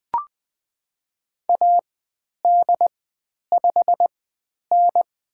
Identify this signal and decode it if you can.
Morse code, heard as a keyed tone, about 20 wpm, spaced as for 10 wpm, 700 Hz. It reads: AD5N
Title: 095 Lesson 16g - Letter c - Call Signs 20wpm